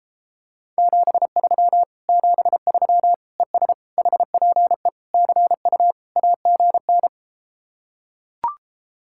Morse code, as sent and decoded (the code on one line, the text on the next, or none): --... ...-- / --... ...-- / . ... / .... .--. / -.-. ..- / .- --. -.
73 73 ES HPE CU AGN